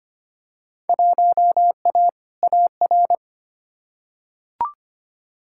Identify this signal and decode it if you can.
Morse code, heard as a keyed tone, about 25 wpm, 700 Hz. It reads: 1A AR
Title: Field Day Exchanges Encore - 25wpm